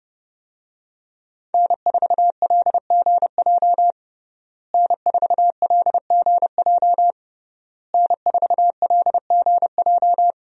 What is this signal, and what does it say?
Morse code, heard as a keyed tone, about 30 wpm, 700 Hz. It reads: N4LGJ N4LGJ N4LGJ
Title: USA Call Signs - 5 Characters - Repeated 3x - 30wpm